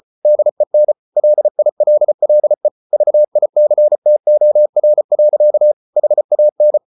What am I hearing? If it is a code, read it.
Morse code, heard as a keyed tone, about 34 wpm, 600 Hz. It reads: DEN LILLE VICTOR. HAN
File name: Ferie QRQ 34wpm